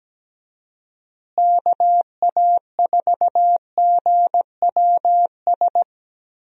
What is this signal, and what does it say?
Morse code, heard as a keyed tone, about 17 wpm, 700 Hz. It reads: KA4GWS
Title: USA Call Signs - 6 Characters - Repeated 2x - 17wpm